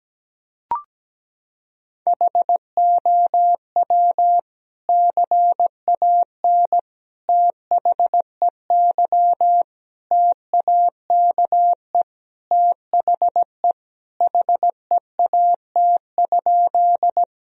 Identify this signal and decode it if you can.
Morse code, heard as a keyed tone, about 17 wpm, 700 Hz. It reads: HOW CAN THEY TAKE THE HEAT?
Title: Sentences from Top 1000 Words - Review Part 3 - 17wpm